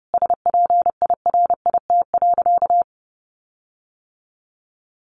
Morse code, heard as a keyed tone, about 30 wpm, 700 Hz. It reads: SPIRIT.
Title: Sentences from Top 1400 Words 30wpm